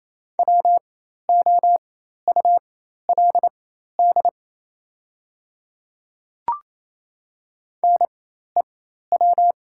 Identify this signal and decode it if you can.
Morse code, heard as a keyed tone, about 28 wpm, spaced as for 13 wpm, 700 Hz. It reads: WOULD NEW
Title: 4x Character Spacing - Top 200 Words - 28wpm